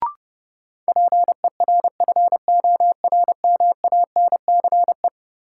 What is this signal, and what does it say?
Morse code, heard as a keyed tone, about 30 wpm, 700 Hz. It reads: PERFORMANCE
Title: Sentences from Top 1000 Words - New Word Drill - 30wpm